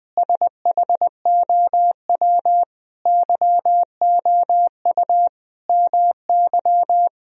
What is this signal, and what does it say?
Morse code, heard as a keyed tone, about 21 wpm, 700 Hz. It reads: SHOW YOU MY
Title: Sentences from Top 1000 Words - Review Part 2 - 20wpm